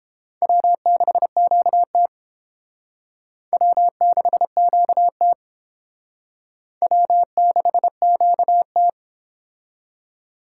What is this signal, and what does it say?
Morse code, heard as a keyed tone, about 30 wpm, 700 Hz. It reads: W6QT W6QT W6QT